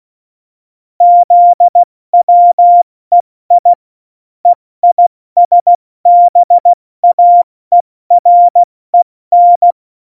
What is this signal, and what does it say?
Morse code, heard as a keyed tone, about 16 wpm, 700 Hz. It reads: ZWEI EISBAEREN
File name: Morsekurs / Witze / Witz 24.Juli 2018, 70 F80